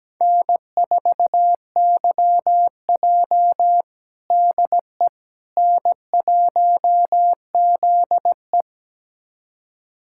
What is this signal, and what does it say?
Morse code, heard as a keyed tone, about 17 wpm, 700 Hz. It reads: N4YJ DE N1ZE
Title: USA Callsign DE Callsign - 4 Characters - 17wpm